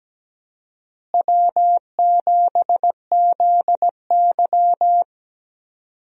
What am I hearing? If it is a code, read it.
Morse code, heard as a keyed tone, about 17 wpm, 700 Hz. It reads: W7ZY
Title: USA Call Signs - 4 Characters - Repeated 3x - 17wpm